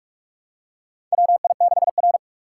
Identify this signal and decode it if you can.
Morse code, heard as a keyed tone, about 41 wpm, 700 Hz. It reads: WI6R
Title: USA Call Signs - 4 Characters - 6x Character Spacing - 45wpm